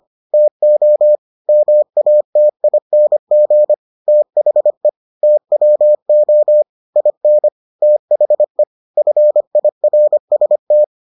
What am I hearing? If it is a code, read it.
Morse code, heard as a keyed tone, about 25 wpm, 600 Hz. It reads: TO MATING THE TWO IN THE FIRST